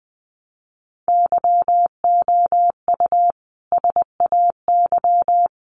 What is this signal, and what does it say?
Morse code, heard as a keyed tone, about 20 wpm, 700 Hz. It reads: YOU SAY
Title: Sentences from Top 700 Words 20wpm